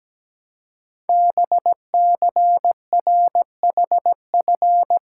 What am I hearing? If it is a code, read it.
Morse code, heard as a keyed tone, about 17 wpm, 700 Hz. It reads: BCRHF
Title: Code Groups - 5 Letters - 17wpm